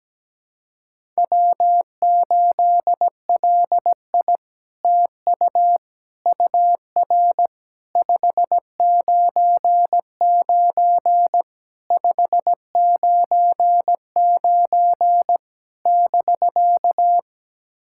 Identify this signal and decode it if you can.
Morse code, heard as a keyed tone, about 17 wpm, 700 Hz. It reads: W8LI TU UR 599 599 <BK>
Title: SOTA QSO Practice 17wpm